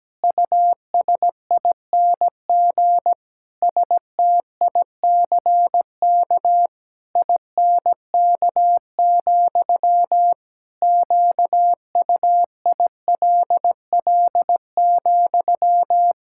Morse code, heard as a keyed tone, about 17 wpm, 700 Hz. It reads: USING STICK INK, QUILL,
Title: Pangram Challenge v2 - 17wpm